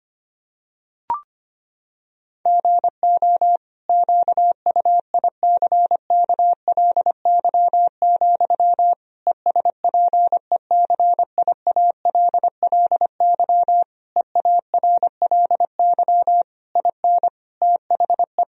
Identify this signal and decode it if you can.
Morse code, heard as a keyed tone, about 25 wpm, 700 Hz. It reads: GO QUICKLY, ESPECIALLY EARLY IN THE
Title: Sentences from Top 1000 Words - Review Part 4 - 25wpm